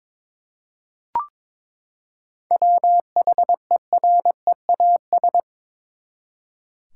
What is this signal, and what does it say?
Morse code, heard as a keyed tone, about 22 wpm, 700 Hz. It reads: WHEREAS
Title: Seven Letter Words - 22wpm